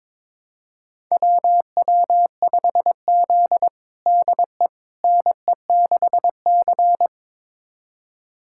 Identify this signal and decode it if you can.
Morse code, heard as a keyed tone, about 22 wpm, 700 Hz. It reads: WW5Z DE NE6C
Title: USA Callsign DE Callsign - 4 Characters - 22wpm